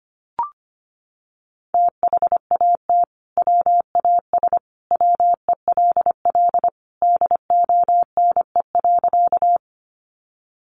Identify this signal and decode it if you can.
Morse code, heard as a keyed tone, about 25 wpm, 700 Hz. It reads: THAT WAS WELL DONE.
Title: Sentences from Top 500 Words 25wpm